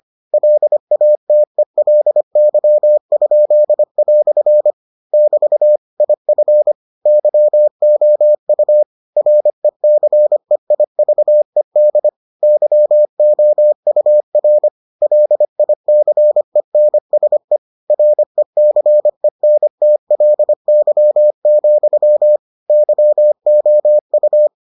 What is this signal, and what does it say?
Morse code, heard as a keyed tone, about 25 wpm, 600 Hz. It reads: LATELY?" <BT> IF YOU RECEIVED YOUR LICENSE RECENTLY, YOU